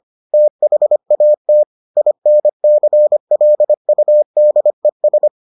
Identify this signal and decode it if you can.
Morse code, heard as a keyed tone, about 25 wpm, 600 Hz. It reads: THAT INCLUDES